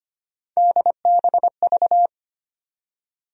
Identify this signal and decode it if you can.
Morse code, heard as a keyed tone, about 25 wpm, 700 Hz. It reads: DBV
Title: USA Call Sign Postfixes - 3 Letters 25wpm